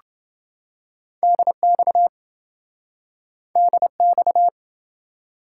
Morse code, heard as a keyed tone, about 27 wpm, 700 Hz. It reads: DX DX